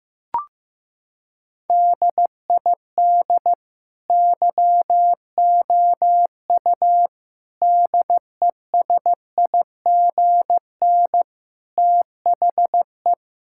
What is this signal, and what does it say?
Morse code, heard as a keyed tone, about 15 wpm, 700 Hz. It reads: DID YOU DESIGN THE
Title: Sentences from Top 1000 Words - Review Part 5 - 15wpm